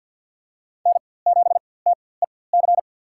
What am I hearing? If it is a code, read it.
Morse code, heard as a keyed tone, about 47 wpm, spaced as for 23 wpm, 700 Hz. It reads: N7TEC
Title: USA Call Signs - 5 Characters - 4x Character Spacing - 50wpm